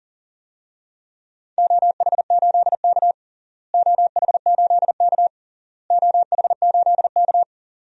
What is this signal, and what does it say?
Morse code, heard as a keyed tone, about 40 wpm, 700 Hz. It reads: OH8K OH8K OH8K
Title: DXCC-100 Call Signs - 4 Characters - Repeated 3x - 40wpm